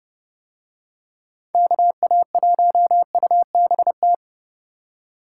Morse code, heard as a keyed tone, about 32 wpm, 700 Hz. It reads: KA1UBT